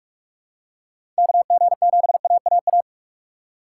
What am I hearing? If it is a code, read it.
Morse code, heard as a keyed tone, about 41 wpm, 700 Hz. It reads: KG7AAA
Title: USA Call Signs - 3x Character Spacing - 45wpm